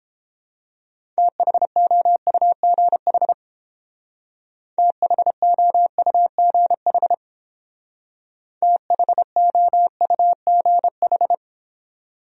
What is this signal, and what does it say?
Morse code, heard as a keyed tone, about 31 wpm, 700 Hz. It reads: THOUGH THOUGH THOUGH